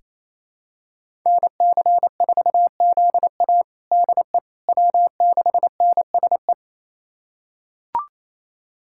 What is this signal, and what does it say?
Morse code, heard as a keyed tone, about 28 wpm, 700 Hz. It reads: NC4ZA DE W6NSE
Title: USA Callsign DE Callsign - 5 Characters - 28wpm